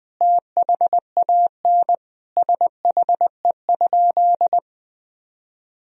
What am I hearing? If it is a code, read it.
Morse code, heard as a keyed tone, about 20 wpm, 700 Hz. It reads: THAN SHE?
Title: Sentences from Top 1000 Words - Review Part 3 - 20wpm